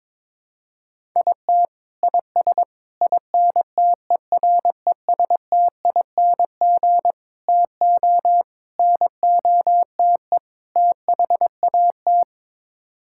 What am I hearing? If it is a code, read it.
Morse code, heard as a keyed tone, about 22 wpm, 700 Hz. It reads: IT IS INTERESTING TO NOTE THAT
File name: practice / 22wpm / Sets of 6 Words - 22wpm